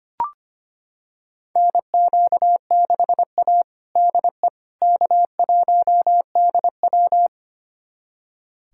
Morse code, heard as a keyed tone, about 25 wpm, 700 Hz. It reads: NQ6A DE K1DW
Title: USA Callsign DE Callsign - 4 Characters - 25wpm